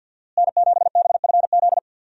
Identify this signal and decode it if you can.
Morse code, heard as a keyed tone, about 53 wpm, 700 Hz. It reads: N7BFZ
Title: USA Call Signs - 5 Characters - Repeated 2x - 50wpm